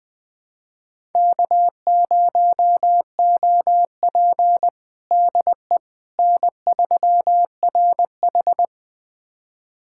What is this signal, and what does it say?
Morse code, heard as a keyed tone, about 20 wpm, 700 Hz. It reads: K0OP DE N3RH
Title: USA Callsign DE Callsign - 4 Characters - 20wpm